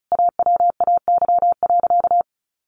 Morse code, heard as a keyed tone, about 34 wpm, 700 Hz. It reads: AWAY.
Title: Sentences from Top 600 Words 35wpm